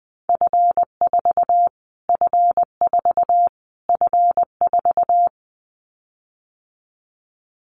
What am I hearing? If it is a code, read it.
Morse code, heard as a keyed tone, about 20 wpm, 700 Hz. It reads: F4 F4 F4
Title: Callsign Prefixes for Top 25 DXCC - Repeated 3x - 20wpm